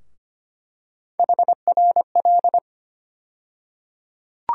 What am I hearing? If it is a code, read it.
Morse code, heard as a keyed tone, about 25 wpm, 700 Hz. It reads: HRL
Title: USA Call Sign Postfixes - 3 Letters 25wpm